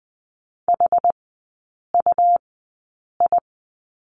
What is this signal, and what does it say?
Morse code, heard as a keyed tone, about 20 wpm, spaced as for 8 wpm, 700 Hz. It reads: HUI
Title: 114 Lesson 19e - Letter w - Three Characters 20wpm